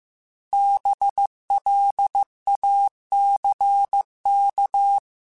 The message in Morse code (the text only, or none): BLACK